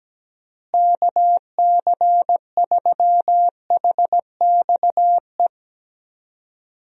Morse code, heard as a keyed tone, about 17 wpm, 700 Hz. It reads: KC3HXE
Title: USA Call Signs - 6 Characters - Repeated 3x - 17wpm